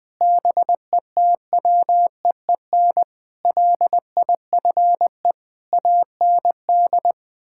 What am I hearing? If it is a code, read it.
Morse code, heard as a keyed tone, about 20 wpm, 700 Hz. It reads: BETWEEN LIFE AND
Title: Sentences with 8-Word Phrases - Spoken Beginning - 20wpm